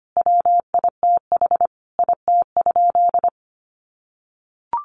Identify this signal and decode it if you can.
Morse code, heard as a keyed tone, about 25 wpm, 700 Hz. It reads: WITH IT?
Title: Sentences from Top 1000 Words 25wpm